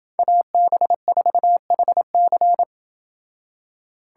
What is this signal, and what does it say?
Morse code, heard as a keyed tone, about 28 wpm, 700 Hz. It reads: AB4HC